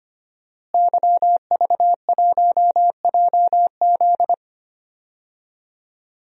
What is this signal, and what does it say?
Morse code, heard as a keyed tone, about 25 wpm, 700 Hz. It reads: YV1JZ
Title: DXCC-100 Call Signs - 5 Characters - Repeated 3x - 25wpm